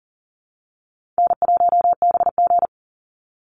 Morse code, heard as a keyed tone, about 40 wpm, 700 Hz. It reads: N1BG
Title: Easy Call Signs Repeated 2x - 40wpm